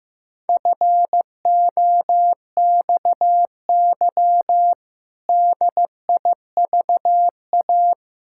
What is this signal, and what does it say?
Morse code, heard as a keyed tone, about 15 wpm, 700 Hz. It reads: FOXY DIVA
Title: Pangram Challenge v2 - 15wpm